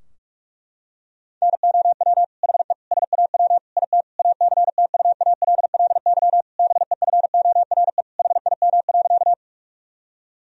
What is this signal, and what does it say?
Morse code, heard as a keyed tone, about 46 wpm, 700 Hz. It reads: NOW HE SAW IT ACTUALLY BEFORE HIM.
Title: Sentences from Top 1000 Words - Review Part 4 - 45wpm